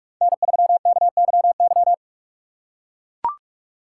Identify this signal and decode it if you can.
Morse code, heard as a keyed tone, about 42 wpm, 700 Hz. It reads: N3KYY
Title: USA Call Signs - 5 Characters - 3x Character Spacing - 45wpm